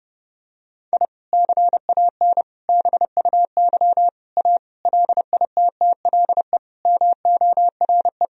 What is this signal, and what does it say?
Morse code, heard as a keyed tone, about 30 wpm, 700 Hz. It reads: I CAN BUY A LITTLE MORE
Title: Sentences from Top 1000 Words - Review Part 1 - 30wpm